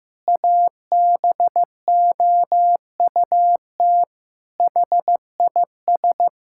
Morse code, heard as a keyed tone, about 16 wpm, 700 Hz. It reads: ABOUT HIS
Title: Pangram Challenge v2 - 15wpm